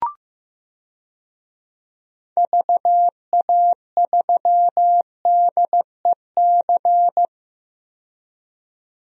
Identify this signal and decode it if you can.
Morse code, heard as a keyed tone, about 16 wpm, 700 Hz. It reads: VA3DEC
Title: DXCC-100 Call Signs - 6 Characters - 15wpm